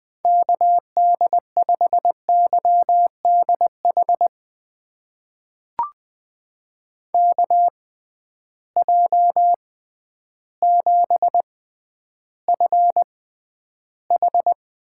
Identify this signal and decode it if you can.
Morse code, heard as a keyed tone, about 20 wpm, spaced as for 11 wpm, 700 Hz. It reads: KD5YDH KJ7FH
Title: USA Call Signs - 6 Characters - 6x Character Spacing - 20wpm